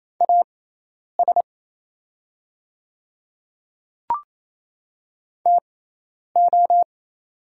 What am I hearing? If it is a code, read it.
Morse code, heard as a keyed tone, about 28 wpm, spaced as for 10 wpm, 700 Hz. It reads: AS TO
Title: 6x Character Spacing - Top 100 Words - 28wpm